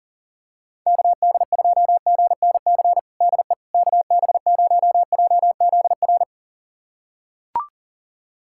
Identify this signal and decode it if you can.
Morse code, heard as a keyed tone, about 40 wpm, 700 Hz. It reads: KD2GNC DE KB0JZR
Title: USA Callsign DE Callsign - 6 Characters - 40wpm